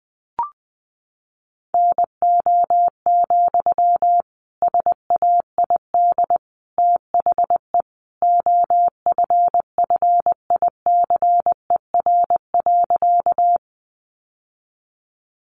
Morse code, heard as a keyed tone, about 20 wpm, 700 Hz. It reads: NO, SAID THE OFFICER.
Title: Sentences from Top 1200 Words 20wpm